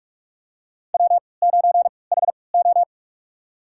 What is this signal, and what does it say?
Morse code, heard as a keyed tone, about 47 wpm, spaced as for 26 wpm, 700 Hz. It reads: W9HO
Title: USA Call Signs - 4 Characters - 3x Character Spacing - 45wpm